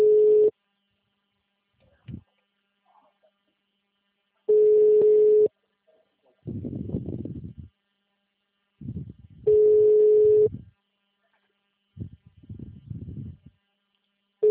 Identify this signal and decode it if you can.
Ringback tone (Europe, 425 Hz).